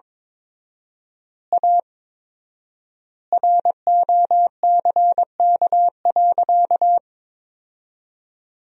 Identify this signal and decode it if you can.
Morse code, heard as a keyed tone, about 23 wpm, 700 Hz. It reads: A ROCK.
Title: 4x Word-Spacing - Sentences from Top 400 Words - 22wpm